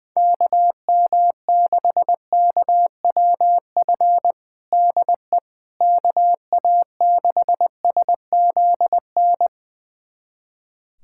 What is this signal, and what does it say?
Morse code, heard as a keyed tone, about 20 wpm, 700 Hz. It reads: KM6KWF DE KA6SZN
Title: USA Callsign DE Callsign - 6 Characters - 20wpm